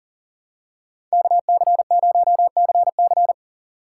Morse code, heard as a keyed tone, about 40 wpm, 700 Hz. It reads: KC0CC
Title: USA Call Signs - 5 Characters - 4x Character Spacing - 40wpm